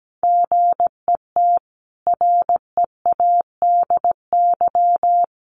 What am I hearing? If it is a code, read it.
Morse code, heard as a keyed tone, about 17 wpm, 700 Hz. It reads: GET READY
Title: Sentences from Top 1200 Words 17wpm